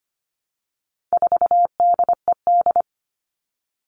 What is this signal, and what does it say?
Morse code, heard as a keyed tone, about 25 wpm, 700 Hz. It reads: 4DED
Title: All Cut Numbers - 4-Digit 25wpm